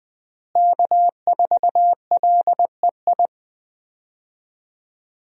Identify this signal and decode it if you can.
Morse code, heard as a keyed tone, about 20 wpm, 700 Hz. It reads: K4LEI
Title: USA Call Signs - 5 Characters - Repeated 3x - 20wpm